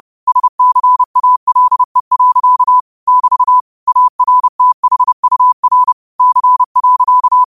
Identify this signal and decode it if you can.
Morse code, heard as a keyed tone, about 30 wpm, 1000 Hz. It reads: SCALE. <BT> ARTHUR C.
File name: qotdfiles / 30wpm